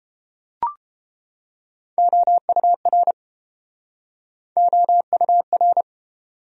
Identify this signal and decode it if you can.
Morse code, heard as a keyed tone, about 29 wpm, 700 Hz. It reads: OUR OUR